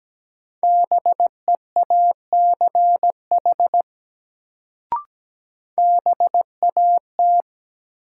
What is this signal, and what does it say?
Morse code, heard as a keyed tone, about 17 wpm, 700 Hz. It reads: BEACH BAT
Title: Sentences from Top 1500 Words - New Word Drill Alt - 17wpm